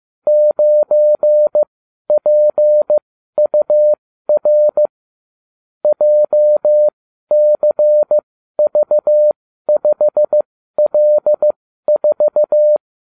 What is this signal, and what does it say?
Morse code, heard as a keyed tone, about 15 wpm, spaced as for 12 wpm, 600 Hz. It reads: 9PUR JCV5L4